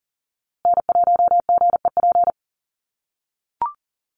Morse code, heard as a keyed tone, about 37 wpm, 700 Hz. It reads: N1GEP
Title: USA Call Signs Repeated 3x 40wpm